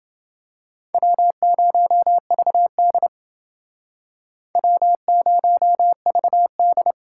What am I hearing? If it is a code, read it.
Morse code, heard as a keyed tone, about 28 wpm, 700 Hz. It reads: W0VD W0VD